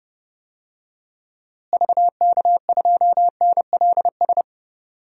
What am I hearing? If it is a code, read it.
Morse code, heard as a keyed tone, about 30 wpm, 700 Hz. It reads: VK2NLS
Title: DXCC-100 Call Signs - 6 Characters - Repeated 3x - 30wpm